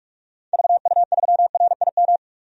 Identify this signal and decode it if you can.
Morse code, heard as a keyed tone, about 46 wpm, 700 Hz. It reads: VU3RIM
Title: DXCC-100 Call Signs - 6 Characters - 45wpm